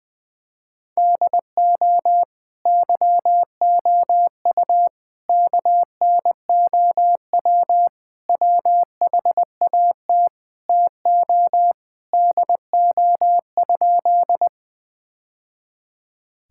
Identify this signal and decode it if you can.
Morse code, heard as a keyed tone, about 20 wpm, 700 Hz. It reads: DO YOU KNOW WHAT TO DO?
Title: Sentences with 2-Word Phrases - 20wpm